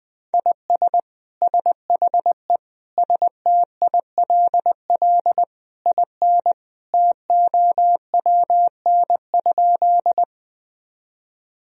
Morse code, heard as a keyed tone, about 20 wpm, 700 Hz. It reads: IS SHE STILL IN TOWN?
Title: Sentences from Top 1000 Words - Review Part 2 - 20wpm